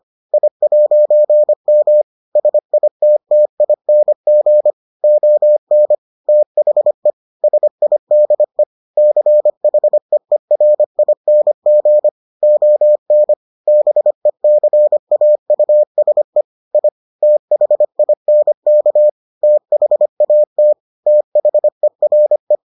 Morse code, heard as a keyed tone, about 25 wpm, 600 Hz. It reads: I'M SITTING ON THE SIDE CHEERING ON BECAUSE I THINK THAT THERE